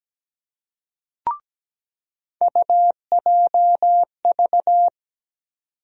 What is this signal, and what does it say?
Morse code, heard as a keyed tone, about 17 wpm, 700 Hz. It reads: UJV
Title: USA Call Sign Postfixes - 3 Letters 17wpm